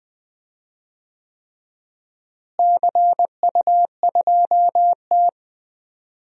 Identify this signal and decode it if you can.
Morse code, heard as a keyed tone, about 20 wpm, 700 Hz. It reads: CU2T
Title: DXCC-100 Call Signs - 4 Characters - Repeated 3x - 20wpm